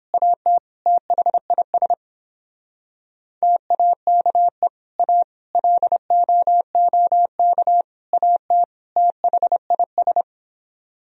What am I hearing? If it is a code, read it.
Morse code, heard as a keyed tone, about 31 wpm, 700 Hz. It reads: AT THIS TAKE A LOOK AT THIS